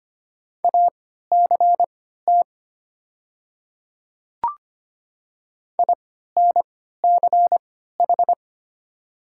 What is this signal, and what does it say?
Morse code, heard as a keyed tone, about 25 wpm, spaced as for 14 wpm, 700 Hz. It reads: ACT INCH
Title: 3x Character Spacing - Top 500 Words - 25wpm